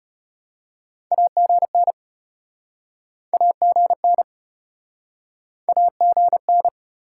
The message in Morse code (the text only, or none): AGN AGN AGN